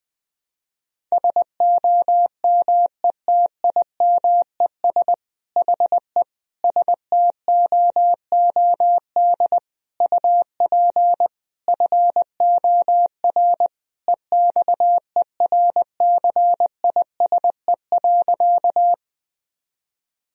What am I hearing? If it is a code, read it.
Morse code, heard as a keyed tone, about 20 wpm, 700 Hz. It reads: SOMETIMES HE STOOD UP FOR EXERCISE.
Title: Sentences from Top 1000 Words - Review Part 4 - 20wpm